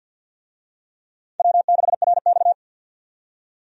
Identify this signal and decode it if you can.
Morse code, heard as a keyed tone, about 53 wpm, 700 Hz. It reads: W6RX
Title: USA Call Signs - 4 Characters - 50wpm